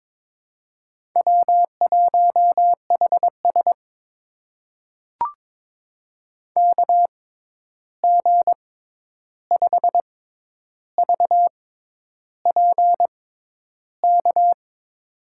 Morse code, heard as a keyed tone, about 22 wpm, spaced as for 10 wpm, 700 Hz. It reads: W1HS KG5VPK
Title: USA Call Signs - 6x Character Spacing - 22wpm